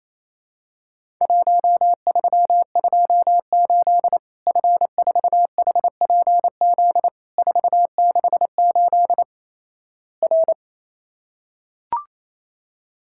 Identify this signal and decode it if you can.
Morse code, heard as a keyed tone, about 28 wpm, 700 Hz. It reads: 1328 F4HPZ 468 R
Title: Worked All Europe DX Contest - QTC Exchange - 28wpm